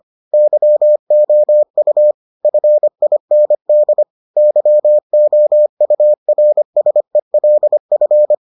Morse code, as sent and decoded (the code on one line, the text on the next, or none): -.-- --- ..- / ..-. .. -. -.. / -.-- --- ..- .-. ... .-.. ..-.
YOU FIND YOURSELF